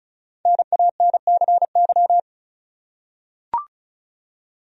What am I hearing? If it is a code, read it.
Morse code, heard as a keyed tone, about 34 wpm, 700 Hz. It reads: NANCY
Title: USA Common Names - 35wpm